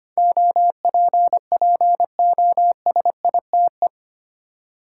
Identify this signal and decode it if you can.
Morse code, heard as a keyed tone, about 25 wpm, 700 Hz. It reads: OPPOSITE
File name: Eight Letter Words - 25wpm